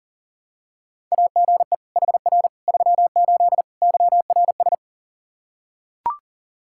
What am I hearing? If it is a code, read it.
Morse code, heard as a keyed tone, about 37 wpm, 700 Hz. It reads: AGE HR 38 YRS
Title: Practice QSO - Exchange 3 - More Info - 40wpm